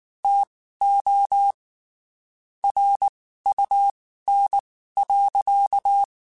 Morse code, heard as a keyed tone, about 19 wpm, spaced as for 14 wpm, 800 Hz. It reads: TO RUN.
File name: (spaced) / QSOs Spaced-mp3s / WA7QOX 2 @18 sp